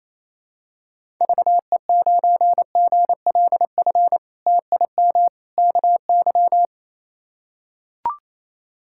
Morse code, heard as a keyed tone, about 28 wpm, 700 Hz. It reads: VE9GLF TIM KY
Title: K1USN SST QSO Practice - 28wpm